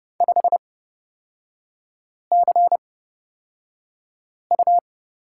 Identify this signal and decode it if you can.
Morse code, heard as a keyed tone, about 30 wpm, spaced as for 5 wpm, 700 Hz. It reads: 5CU
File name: Lesson 36e - BT - Three Characters 30wpm